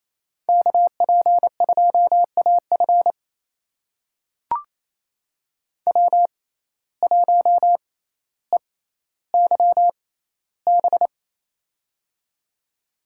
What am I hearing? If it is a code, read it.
Morse code, heard as a keyed tone, about 28 wpm, spaced as for 14 wpm, 700 Hz. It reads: KP2AF W1EYB
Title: USA Call Signs - 5 Characters - 6x Character Spacing - 28wpm